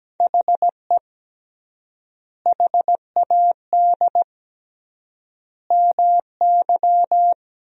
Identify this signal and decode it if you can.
Morse code, heard as a keyed tone, about 17 wpm, 700 Hz. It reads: HE HAD MY
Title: 3x Word-Spacing - Sentences from Top 400 Words - 17wpm